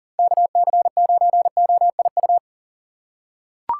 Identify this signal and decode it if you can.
Morse code, heard as a keyed tone, about 40 wpm, 700 Hz. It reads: KC9OIU
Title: USA Call Signs - 6 Characters - 2x Character Spacing - 40wpm